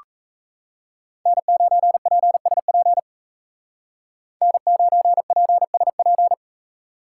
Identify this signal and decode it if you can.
Morse code, heard as a keyed tone, about 33 wpm, 700 Hz. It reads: N9PSP N9PSP
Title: Speed Racing - USA 1x3 Call Signs 28wpm